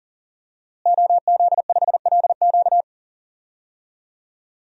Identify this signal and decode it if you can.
Morse code, heard as a keyed tone, about 40 wpm, 700 Hz. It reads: OZ5LQ
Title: DXCC-100 Call Signs - 5 Characters - Repeated 3x - 40wpm